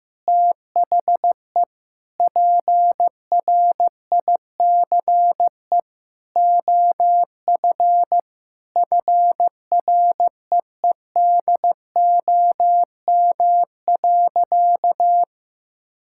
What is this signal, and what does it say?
Morse code, heard as a keyed tone, about 15 wpm, 700 Hz. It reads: THE PRICE OF FREEDOM.
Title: Sentences from Top 1000 Words - Review Part 5 - 15wpm